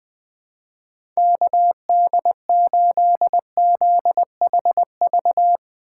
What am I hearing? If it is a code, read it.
Morse code, heard as a keyed tone, about 19 wpm, 700 Hz. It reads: KD8ZHV
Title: Speed Racing - USA 2x3 Call Signs 17wpm